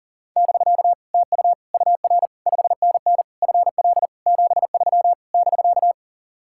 Morse code, heard as a keyed tone, about 50 wpm, 700 Hz. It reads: <BK> TU UR 5NN FL 73 <BK>
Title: Speed-Racing Common QSO Phrases - 40wpm